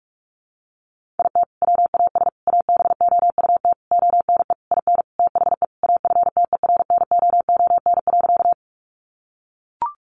Morse code, heard as a keyed tone, about 44 wpm, 700 Hz. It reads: IT WAS ABOUT ONE IN THE AFTERNOON.
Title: Sentences from Top 1500 Words 45wpm